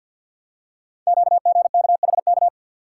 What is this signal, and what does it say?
Morse code, heard as a keyed tone, about 50 wpm, 700 Hz. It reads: QGKHK
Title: Code Groups - 5 Letters - 50wpm